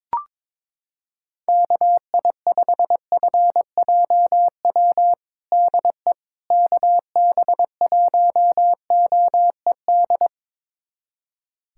Morse code, heard as a keyed tone, about 21 wpm, 700 Hz. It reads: KI5FJW DE KB1OED
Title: USA Callsign DE Callsign - 6 Characters - 22wpm